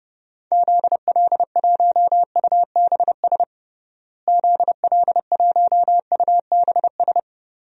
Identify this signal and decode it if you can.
Morse code, heard as a keyed tone, about 30 wpm, 700 Hz. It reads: ZL1UBS ZL1UBS
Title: DXCC-100 Call Signs - 6 Characters - Repeated 3x - 30wpm